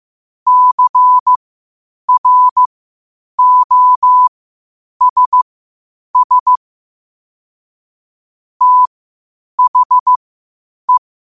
Morse code, heard as a keyed tone, about 15 wpm, spaced as for 8 wpm, 1000 Hz. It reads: CROSS THE
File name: qotdfiles / 8wpm